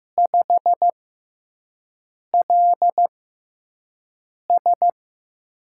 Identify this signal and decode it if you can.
Morse code, heard as a keyed tone, about 15 wpm, spaced as for 5 wpm, 700 Hz. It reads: 5LS
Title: USA Call Signs - 4 Characters - 6x Character Spacing - 15wpm